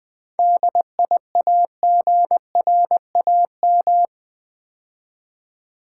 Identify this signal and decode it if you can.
Morse code, heard as a keyed tone, about 20 wpm, 700 Hz. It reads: DIAGRAM
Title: 7 Letter Words - First 3 Letters Given - 20wpm